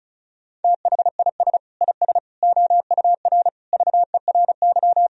Morse code, heard as a keyed tone, about 34 wpm, 700 Hz. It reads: THIS IS OUR VERY
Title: Sentences from Top 1000 Words - Review Part 4 - 35wpm